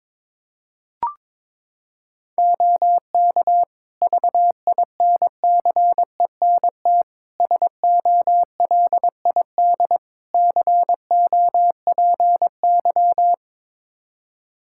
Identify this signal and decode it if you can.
Morse code, heard as a keyed tone, about 21 wpm, 700 Hz. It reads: OK VINCENT SOLID COPY
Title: Practice QSO - Exchange 3 - More Info - 22wpm